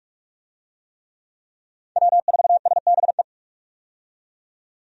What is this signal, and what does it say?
Morse code, heard as a keyed tone, about 42 wpm, 700 Hz. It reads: W4SBE
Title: USA Call Signs - 5 Characters - 45wpm